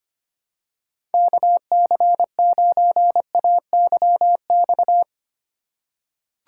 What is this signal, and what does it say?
Morse code, heard as a keyed tone, about 25 wpm, 700 Hz. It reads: KC9AYX
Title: USA Call Signs - 6 Characters - 2x Character Spacing - 25wpm